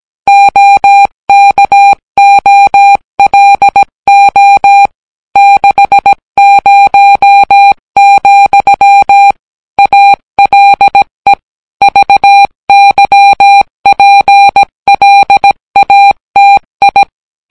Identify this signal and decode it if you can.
Morse code, heard as a keyed tone, about 17 wpm, 800 Hz. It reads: OKOLO 60, ALE VYPLATI